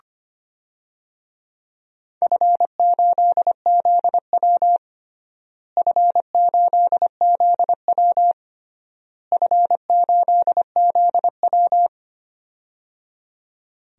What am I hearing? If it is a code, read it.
Morse code, heard as a keyed tone, about 25 wpm, 700 Hz. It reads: F8ZW F8ZW F8ZW
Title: DXCC-100 Call Signs - 4 Characters - Repeated 3x - 25wpm